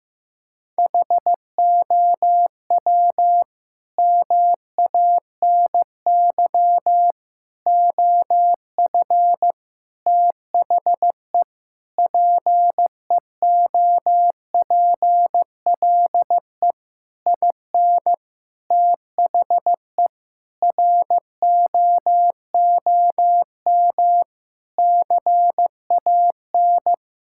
Morse code, heard as a keyed tone, about 15 wpm, 700 Hz. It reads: HOW MANY OF THE PEOPLE IN THE ROOM CAN
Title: Sentences from Top 1000 Words - Review Part 6 - 15wpm